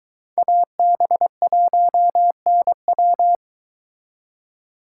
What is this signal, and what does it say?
Morse code, heard as a keyed tone, about 22 wpm, 700 Hz. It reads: AB1NW